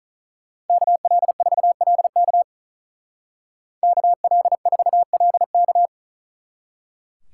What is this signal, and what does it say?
Morse code, heard as a keyed tone, about 43 wpm, 700 Hz. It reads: KL4LK KL4LK
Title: Speed Racing - USA Call Signs 35wpm